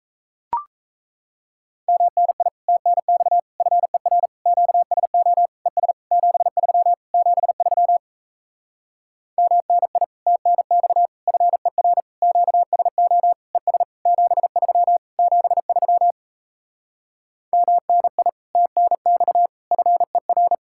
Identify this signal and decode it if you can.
Morse code, heard as a keyed tone, about 38 wpm, 700 Hz. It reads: MNI TNX FER QSO ES 73 73 MNI TNX FER QSO ES 73 73 MNI TNX FER